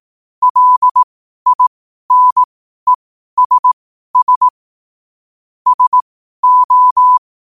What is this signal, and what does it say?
Morse code, heard as a keyed tone, about 18 wpm, spaced as for 12 wpm, 1000 Hz. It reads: LINESS SO
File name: qotdfiles / 12wpm